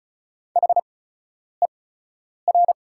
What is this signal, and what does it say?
Morse code, heard as a keyed tone, about 34 wpm, spaced as for 10 wpm, 690 Hz. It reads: HER
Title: 8x Character Spacing - Top 200 Words - 35wpm